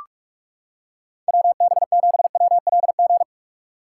Morse code, heard as a keyed tone, about 38 wpm, 700 Hz. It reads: WB7WLG